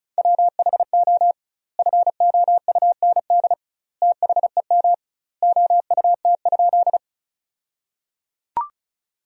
Morse code, heard as a keyed tone, about 36 wpm, 700 Hz. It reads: WHO FOUND THEM OUT?
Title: 2x Word-Spacing - Sentences from Top 300 Words - 35wpm